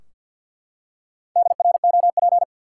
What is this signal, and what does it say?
Morse code, heard as a keyed tone, about 52 wpm, 700 Hz. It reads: DROP
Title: Four Letter Words - 50wpm